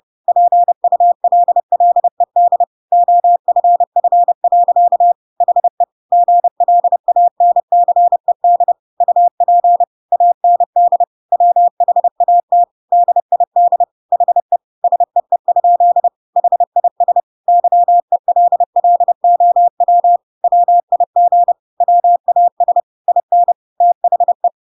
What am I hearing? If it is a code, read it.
Morse code, heard as a keyed tone, about 30 wpm, 700 Hz. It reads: PULLED OFF. HE GLANCED UP AND WHAT DID HE SEE? HIS YELLOW WIG WAS IN THE
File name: Pinocchio / Pinocchio ch3 30 30 ik1hge